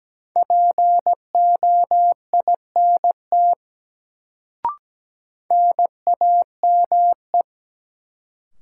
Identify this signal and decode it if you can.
Morse code, heard as a keyed tone, about 17 wpm, 700 Hz. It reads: POINT NAME
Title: Sentences from Top 200 Words - New Word Drill - 17wpm